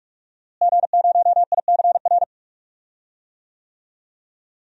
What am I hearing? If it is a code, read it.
Morse code, heard as a keyed tone, about 46 wpm, 700 Hz. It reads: G0ICR
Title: DXCC-100 Call Signs - 5 Characters - Repeated 3x - 45wpm